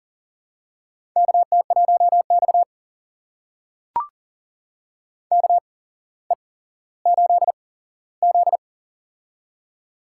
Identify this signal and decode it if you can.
Morse code, heard as a keyed tone, about 40 wpm, spaced as for 17 wpm, 700 Hz. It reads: KT1X KE8Z
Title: USA Call Signs - 4 Characters - 8x Character Spacing - 40wpm